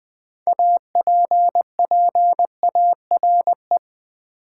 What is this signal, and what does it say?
Morse code, heard as a keyed tone, about 20 wpm, 700 Hz. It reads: APPARE
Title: Build A Word - 3 Letters At A Time - 8-Letter Words - 20wpm